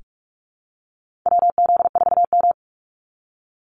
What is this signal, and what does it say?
Morse code, heard as a keyed tone, about 44 wpm, 700 Hz. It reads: WZ4M
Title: NA SOTA Activator Callsigns 45wpm